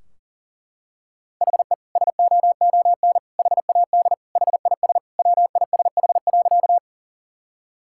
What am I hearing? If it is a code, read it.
Morse code, heard as a keyed tone, about 37 wpm, 700 Hz. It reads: HE SOON HAD HIS WISH.
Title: Sentences from Top 1600 Words 40wpm